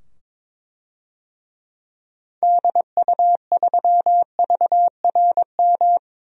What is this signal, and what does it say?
Morse code, heard as a keyed tone, about 21 wpm, 700 Hz. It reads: DU3VRM
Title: DXCC-100 Call Signs - 6 Characters - 22wpm